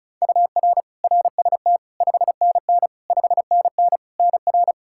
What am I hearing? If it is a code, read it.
Morse code, heard as a keyed tone, about 34 wpm, 700 Hz. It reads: UR RST 5NN 5NN NR
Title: Practice QSO - Exchange 1 - Beginning - 35wpm